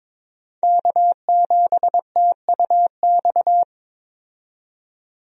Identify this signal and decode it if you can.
Morse code, heard as a keyed tone, about 22 wpm, 700 Hz. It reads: K7TUX
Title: USA Call Signs - 5 Characters - 22wpm